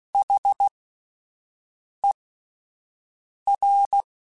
Morse code, heard as a keyed tone, about 16 wpm, spaced as for 5 wpm, 800 Hz. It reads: HER
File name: Alphabetical / Alphabetical mp3s / Hs @5